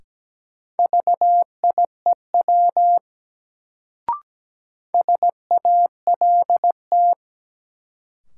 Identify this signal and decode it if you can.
Morse code, heard as a keyed tone, about 17 wpm, 700 Hz. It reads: VIEW SALT
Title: Sets of 5 Words - New Word Drill - 17wpm